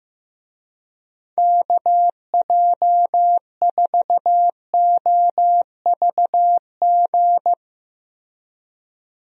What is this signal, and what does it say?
Morse code, heard as a keyed tone, about 15 wpm, 700 Hz. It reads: KJ4OVG
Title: USA Call Signs - 6 Characters - 15wpm